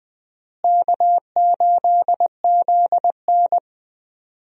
Morse code, heard as a keyed tone, about 20 wpm, 700 Hz. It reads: K8ZN
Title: USA Call Signs - 4 Characters - Repeated 2x - 20wpm